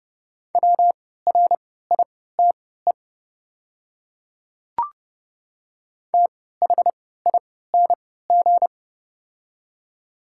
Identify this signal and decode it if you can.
Morse code, heard as a keyed tone, about 31 wpm, spaced as for 17 wpm, 700 Hz. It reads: WRITE THING
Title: 3x Character Spacing - Top 100 Words - 30wpm